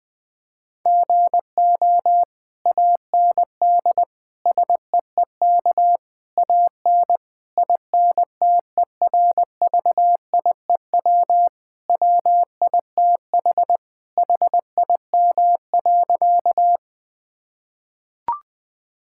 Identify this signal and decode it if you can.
Morse code, heard as a keyed tone, about 20 wpm, 700 Hz. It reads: GO AND SEEK AN INTERVIEW WITH HIM.
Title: Sentences from Top 1600 Words 20wpm